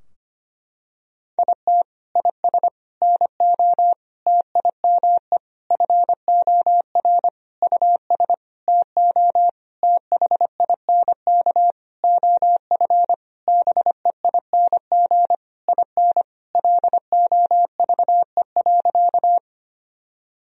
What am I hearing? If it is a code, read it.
Morse code, heard as a keyed tone, about 25 wpm, 700 Hz. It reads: IT IS NO TIME FOR US TO THINK OF BEING IN LOVE.
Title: Sentences from Top 1000 Words - Review Part 3 - 25wpm